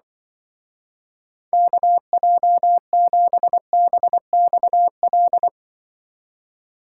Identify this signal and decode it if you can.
Morse code, heard as a keyed tone, about 25 wpm, 700 Hz. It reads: KJ7BXL
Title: Speed Racing - USA 2x3 Call Signs 20wpm